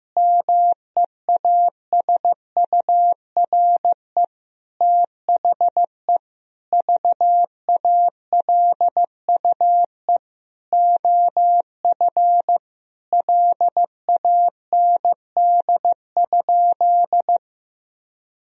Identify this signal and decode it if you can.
Morse code, heard as a keyed tone, about 15 wpm, 700 Hz. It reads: MEASURE THE VALUE OF LAND?
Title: Sentences from Top 1000 Words - Review Part 3 - 15wpm